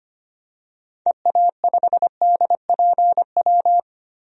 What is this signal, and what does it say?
Morse code, heard as a keyed tone, about 25 wpm, 700 Hz. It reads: EA5DPW
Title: DXCC-100 Call Signs - 6 Characters - 25wpm